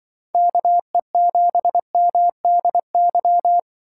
Morse code, heard as a keyed tone, about 24 wpm, 700 Hz. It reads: KE7MDY
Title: Speed Racing - USA 2x3 Call Signs 20wpm